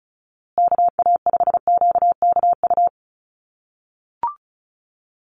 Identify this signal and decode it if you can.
Morse code, heard as a keyed tone, about 34 wpm, 700 Hz. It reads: KA5QKU
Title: USA Call Signs Repeated 2x 35wpm